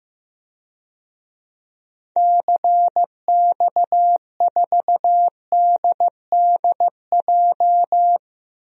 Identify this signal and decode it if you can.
Morse code, heard as a keyed tone, about 15 wpm, 700 Hz. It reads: CX4DDJ
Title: DXCC-100 Call Signs - 6 Characters - 15wpm